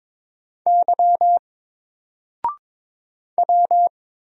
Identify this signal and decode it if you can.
Morse code, heard as a keyed tone, about 22 wpm, 700 Hz. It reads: Y W
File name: Single-Letter-Number-22wpm